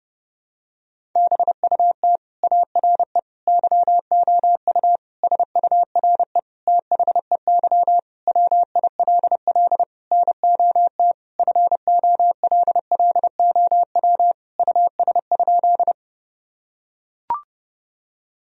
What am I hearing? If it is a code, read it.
Morse code, heard as a keyed tone, about 31 wpm, 700 Hz. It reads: BUT ARE YOU SURE THEY WILL NOT FOLLOW US?
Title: Sentences from Top 1000 Words - Review Part 1 - 30wpm